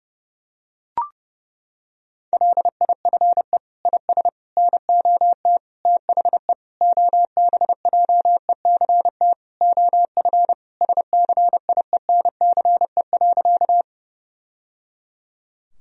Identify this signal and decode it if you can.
Morse code, heard as a keyed tone, about 32 wpm, 700 Hz. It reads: LIFE IS NOT THE OBJECT OF SCIENCE.
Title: Sentences from Top 1000 Words - Review Part 3 - 30wpm